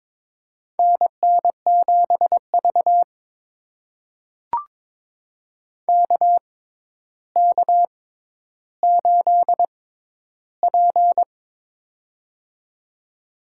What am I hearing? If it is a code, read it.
Morse code, heard as a keyed tone, about 21 wpm, spaced as for 11 wpm, 700 Hz. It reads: NN7V KK8P
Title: USA Call Signs - 6x Character Spacing - 22wpm